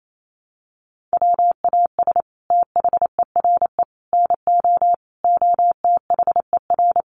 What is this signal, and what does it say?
Morse code, heard as a keyed tone, about 28 wpm, 700 Hz. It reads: WAS THERE NO OTHER
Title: Sentences from Top 1400 Words 28wpm